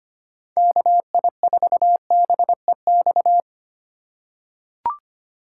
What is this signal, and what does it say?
Morse code, heard as a keyed tone, about 25 wpm, 700 Hz. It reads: KI4BEX
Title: USA Call Signs - 6x Character Spacing - 25wpm